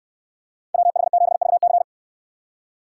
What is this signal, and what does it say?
Morse code, heard as a keyed tone, about 69 wpm, 700 Hz. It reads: WH7VC